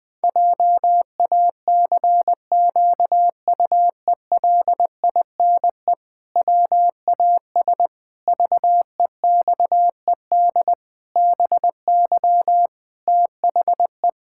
Morse code, heard as a keyed tone, about 20 wpm, 700 Hz. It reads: JACQUELINE WAS VEXED BY THE
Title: Pangram Challenge v2 - 20wpm